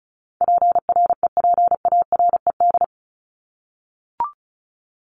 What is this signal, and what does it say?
Morse code, heard as a keyed tone, about 34 wpm, 700 Hz. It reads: PREPARED
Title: ED Suffix - Top 100 Words with Base Words 35wpm